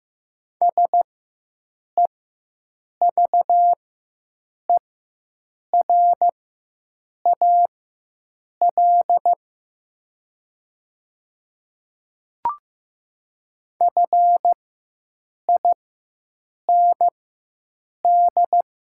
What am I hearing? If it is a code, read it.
Morse code, heard as a keyed tone, about 15 wpm, spaced as for 7 wpm, 700 Hz. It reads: SEVERAL FIND
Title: 4x Character Spacing - Top 500 Words - 15wpm